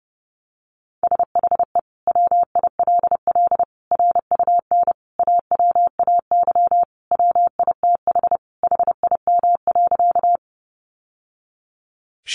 Morse code, heard as a keyed tone, about 30 wpm, 700 Hz. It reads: SHE WILL RUN AWAY WITH HIM.
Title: Sentences from Top 600 Words 30wpm